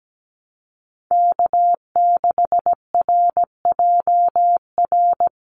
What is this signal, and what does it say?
Morse code, heard as a keyed tone, about 17 wpm, 700 Hz. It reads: K6RJR
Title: USA Call Signs Repeated 2x 17wpm